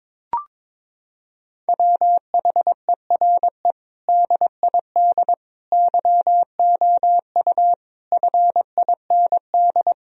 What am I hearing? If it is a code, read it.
Morse code, heard as a keyed tone, about 22 wpm, 700 Hz. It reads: WHERE DID YOU FIND
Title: Sentences from Top 1000 Words - Review Part 2 - 22wpm